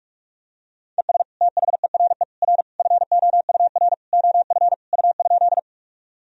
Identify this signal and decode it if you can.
Morse code, heard as a keyed tone, about 42 wpm, 700 Hz. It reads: ES THERE R FOUR OF U?
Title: Sentences from Top 300 Words and Abbreviations 45wpm